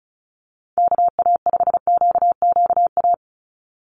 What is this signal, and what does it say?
Morse code, heard as a keyed tone, about 36 wpm, 700 Hz. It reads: KA5QQA
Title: NA SOTA Chaser Callsigns 35wpm